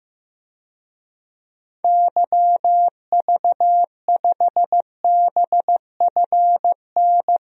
Morse code, heard as a keyed tone, about 15 wpm, 700 Hz. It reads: YV5BFN
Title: DXCC-100 Call Signs - 6 Characters - Repeated 2x - 15wpm